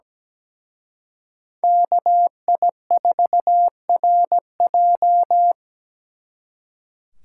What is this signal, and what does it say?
Morse code, heard as a keyed tone, about 21 wpm, 700 Hz. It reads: KI4RJ
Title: Speed Racing - USA 2x2 Call Signs 17wpm